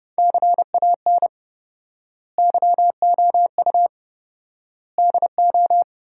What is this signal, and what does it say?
Morse code, heard as a keyed tone, about 30 wpm, 700 Hz. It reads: CAN YOU DO
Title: 4x Word-Spacing - Sentences from Top 100 Words - 30wpm